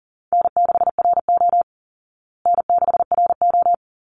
Morse code, heard as a keyed tone, about 40 wpm, 700 Hz. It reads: N6RO N6RO
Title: Easy Call Signs Repeated 3x - 40wpm